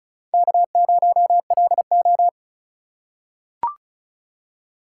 Morse code, heard as a keyed tone, about 36 wpm, 700 Hz. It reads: K0LO
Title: USA Call Signs - 4 Characters - 3x Character Spacing - 35wpm